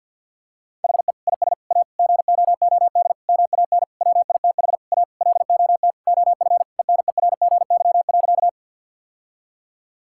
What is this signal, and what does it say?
Morse code, heard as a keyed tone, about 51 wpm, 700 Hz. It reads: HE IS A GOOD MAN WITH A LOT OF ENERGY.
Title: Sentences from Top 1000 Words - Review Part 6 - 50wpm